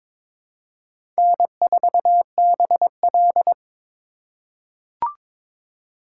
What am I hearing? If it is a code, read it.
Morse code, heard as a keyed tone, about 22 wpm, 700 Hz. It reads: N4BL
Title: USA Call Signs - 4 Characters - 3x Character Spacing - 22wpm